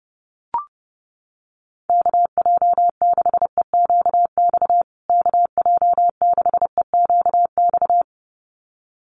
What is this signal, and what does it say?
Morse code, heard as a keyed tone, about 30 wpm, 700 Hz. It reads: KJ6EQX KJ6EQX
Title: USA Call Signs Repeated 2x 30wpm